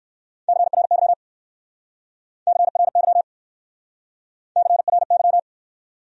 Morse code, heard as a keyed tone, about 60 wpm, 700 Hz. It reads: CRY CRY CRY